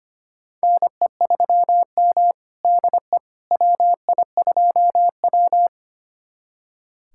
Morse code, heard as a keyed tone, about 25 wpm, 700 Hz. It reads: NE3M DE WI2W
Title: USA Callsign DE Callsign - 4 Characters - 25wpm